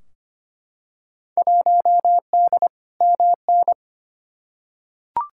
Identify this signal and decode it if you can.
Morse code, heard as a keyed tone, about 25 wpm, 700 Hz. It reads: 1D MN
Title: Field Day Exchanges - 25wpm